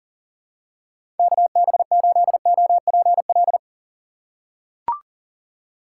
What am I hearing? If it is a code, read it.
Morse code, heard as a keyed tone, about 40 wpm, 700 Hz. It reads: KB8OPL
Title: USA Call Signs - 6 Characters - 6x Character Spacing - 40wpm